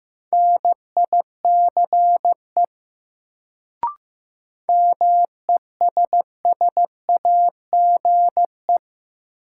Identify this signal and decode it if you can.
Morse code, heard as a keyed tone, about 15 wpm, 700 Hz. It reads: NICE MESSAGE
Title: Sentences from Top 1100 Words - New Word Drill Alt - 15wpm